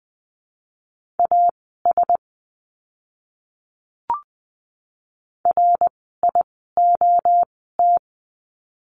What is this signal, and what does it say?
Morse code, heard as a keyed tone, about 20 wpm, spaced as for 15 wpm, 700 Hz. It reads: AS RIOT
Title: 061 Lesson 11f - Letter h - Words 20wpm